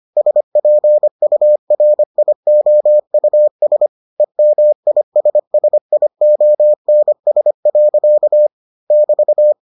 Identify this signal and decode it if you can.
Morse code, heard as a keyed tone, about 25 wpm, 600 Hz. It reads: SPURIOUS EMISSIONS. <BT>